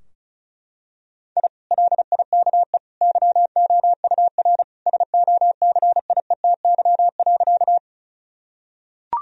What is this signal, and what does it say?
Morse code, heard as a keyed tone, about 36 wpm, 700 Hz. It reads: I LIKE YOUR SOCIETY.
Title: Sentences from Top 1000 Words - Review Part 4 - 35wpm